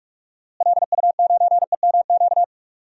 Morse code, heard as a keyed tone, about 41 wpm, 700 Hz. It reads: LU9EMQ
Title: DXCC-100 Call Signs - 6 Characters - Repeated 2x - 45wpm